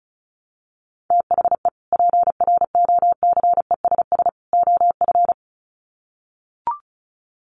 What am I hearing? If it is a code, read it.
Morse code, heard as a keyed tone, about 34 wpm, 700 Hz. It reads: THE PROCESS OF
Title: Sets of 3 Words 35wpm